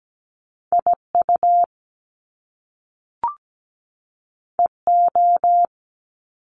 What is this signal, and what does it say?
Morse code, heard as a keyed tone, about 17 wpm, 700 Hz. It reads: IU EO
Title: Top Two Letter Combinations in Words 17wpm